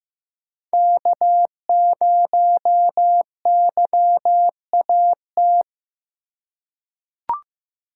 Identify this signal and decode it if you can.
Morse code, heard as a keyed tone, about 15 wpm, 700 Hz. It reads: K0YAT
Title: USA Call Signs - 5 Characters - 2x Character Spacing - 15wpm